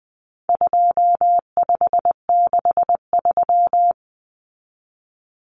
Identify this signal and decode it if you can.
Morse code, heard as a keyed tone, about 20 wpm, 700 Hz. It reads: 2563
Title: Common Cut Numbers - 4-Digit 20wpm